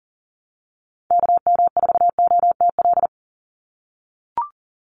Morse code, heard as a keyed tone, about 40 wpm, 700 Hz. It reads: KM4OTL
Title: USA Call Signs 40wpm